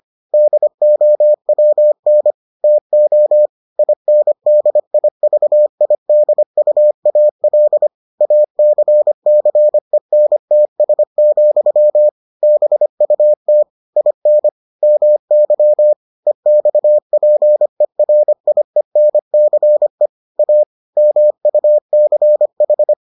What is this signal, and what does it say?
Morse code, heard as a keyed tone, about 25 wpm, 600 Hz. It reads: DOWN TO INDIVIDUAL ACCENTS, BUT IN MY EXPERIENCE A MUCH